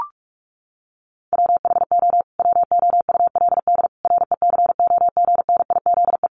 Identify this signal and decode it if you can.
Morse code, heard as a keyed tone, about 44 wpm, 700 Hz. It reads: WHO WOULD RECOGNIZE
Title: Sentences from Top 1400 Words 45wpm